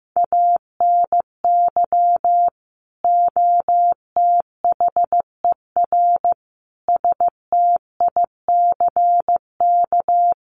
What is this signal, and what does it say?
Morse code, heard as a keyed tone, about 15 wpm, 700 Hz. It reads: ANY OTHER STICK
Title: Sentences from Top 1200 Words 15wpm